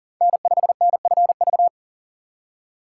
Morse code, heard as a keyed tone, about 40 wpm, 700 Hz. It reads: N5NFV
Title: USA Call Signs - 5 Characters - 2x Character Spacing - 40wpm